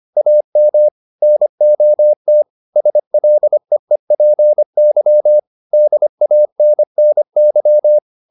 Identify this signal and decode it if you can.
Morse code, heard as a keyed tone, about 25 wpm, 600 Hz. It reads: AM NOT SLEEPY DANNY